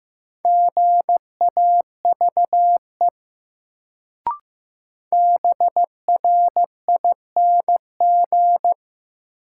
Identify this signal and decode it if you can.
Morse code, heard as a keyed tone, about 15 wpm, 700 Hz. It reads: GAVE BRING
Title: Sentences from Top 500 Words - New Word Drill - 15wpm